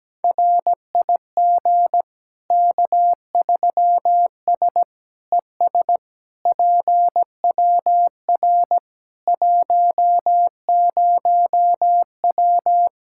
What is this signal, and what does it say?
Morse code, heard as a keyed tone, about 17 wpm, 700 Hz. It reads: RIG K3S ES PWR 10W
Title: Practice QSO - Exchange 2 - Info - 17wpm